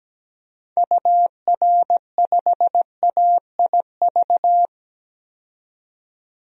Morse code, heard as a keyed tone, about 17 wpm, 700 Hz. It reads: UR5AIV
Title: DXCC-100 Call Signs - 6 Characters - 17wpm